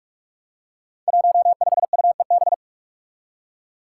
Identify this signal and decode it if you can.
Morse code, heard as a keyed tone, about 46 wpm, 700 Hz. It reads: 15UEB
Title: Five Challenging Letter-Number Combinations 45wpm